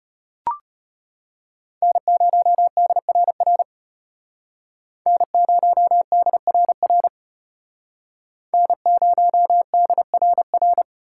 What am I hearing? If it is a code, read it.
Morse code, heard as a keyed tone, about 32 wpm, 700 Hz. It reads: N0DRR N0DRR N0DRR